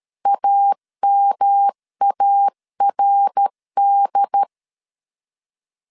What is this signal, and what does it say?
Morse code, heard as a keyed tone, about 13 wpm, 800 Hz. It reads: AMARD
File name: CW / DET / 1 3 1500C S A 12wpm 2905